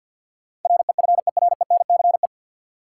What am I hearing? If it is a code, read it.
Morse code, heard as a keyed tone, about 53 wpm, 700 Hz. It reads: REFERENCE
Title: Sentences from Top 1200 Words - New Word Drill Alt - 50wpm